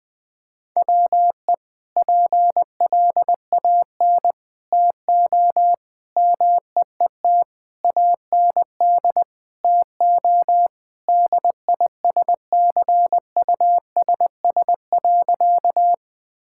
Morse code, heard as a keyed tone, about 20 wpm, 700 Hz. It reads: WE PLAN TO MEET AND TO DISCUSS.
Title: Sentences with 2-Word Phrases - Spoken Beginning - 20wpm